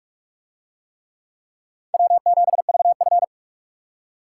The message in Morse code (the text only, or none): W7VF